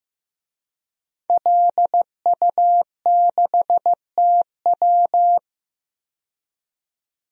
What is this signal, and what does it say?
Morse code, heard as a keyed tone, about 15 wpm, 700 Hz. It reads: LU6TW
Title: DXCC-100 Call Signs - 5 Characters - 15wpm